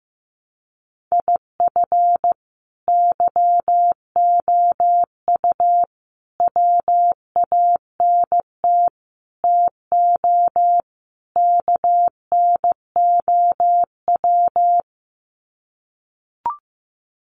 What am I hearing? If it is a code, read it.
Morse code, heard as a keyed tone, about 15 wpm, 700 Hz. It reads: IF YOU WANT TO KNOW
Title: Sets of 5 Words Encore2 15wpm